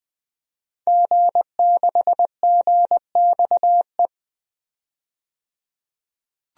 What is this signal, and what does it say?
Morse code, heard as a keyed tone, about 21 wpm, 700 Hz. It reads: G6GXE
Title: DXCC-100 Call Signs - 5 Characters - 20wpm